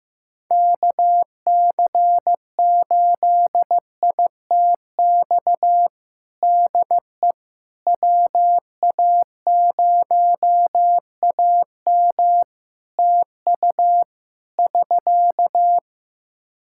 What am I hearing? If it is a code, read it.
Morse code, heard as a keyed tone, about 16 wpm, 700 Hz. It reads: KC8ITX DE WA0AM TU <SK>
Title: Practice QSO - Exchange 4 - Ending - 15wpm